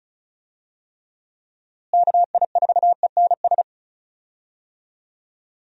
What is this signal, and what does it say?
Morse code, heard as a keyed tone, about 34 wpm, 700 Hz. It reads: KI4ENS
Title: USA Call Signs - 6 Characters - Repeated 3x - 35wpm